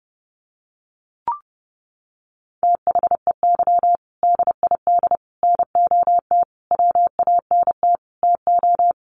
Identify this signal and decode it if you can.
Morse code, heard as a keyed tone, about 30 wpm, 700 Hz. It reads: THEY DID NOT WANT TO
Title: Sets of 5 Words 30wpm